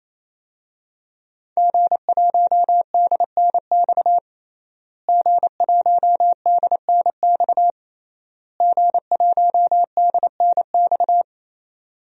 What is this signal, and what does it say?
Morse code, heard as a keyed tone, about 28 wpm, 700 Hz. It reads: G1DNX G1DNX G1DNX
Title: DXCC-100 Call Signs - 5 Characters - Repeated 3x - 28wpm